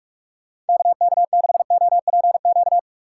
Morse code, heard as a keyed tone, about 46 wpm, 690 Hz. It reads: KK6OPQ
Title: USA Call Signs - 6 Characters - 3x Character Spacing - 45wpm